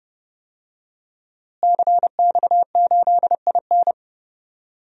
Morse code, heard as a keyed tone, about 30 wpm, 700 Hz. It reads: CX8IN
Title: DXCC-100 Call Signs - 5 Characters - Repeated 2x - 30wpm